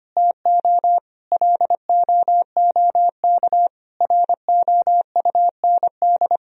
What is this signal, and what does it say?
Morse code, heard as a keyed tone, about 25 wpm, 700 Hz. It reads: TO LOOK ROUND
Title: Sentences from Top 200 Words and Abbreviations 25wpm